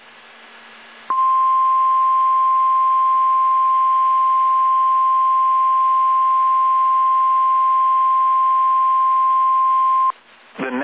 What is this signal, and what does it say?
Emergency Alert System (SAME) tones (the 1050 Hz attention tone). It is heard from 1.0 to 10.1 s.